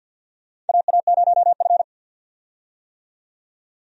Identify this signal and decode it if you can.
Morse code, heard as a keyed tone, about 52 wpm, 700 Hz. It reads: AA0F
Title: USA Call Signs - 4 Characters - 50wpm